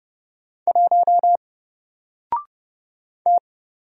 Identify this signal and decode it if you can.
Morse code, heard as a keyed tone, about 30 wpm, 700 Hz. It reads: 1 T
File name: Single-Letter-Number-30wpm